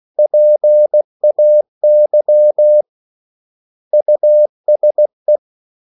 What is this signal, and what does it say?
Morse code, heard as a keyed tone, about 16 wpm, 600 Hz. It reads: PAY USE
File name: CW / Practice / 500MostCommonEnglishWords16